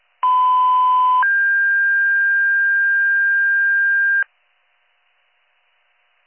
Two-tone page, heard as a one-second tone, then a three-second tone, about 1010 then 1600 Hz. It is heard from 0.2 to 4.2 s.